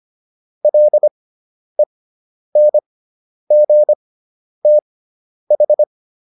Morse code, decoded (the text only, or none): LENGTH